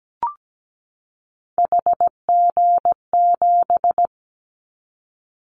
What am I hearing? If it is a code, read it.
Morse code, heard as a keyed tone, about 17 wpm, 700 Hz. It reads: HG7
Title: Callsign Prefixes for Top 25 DXCC - 17wpm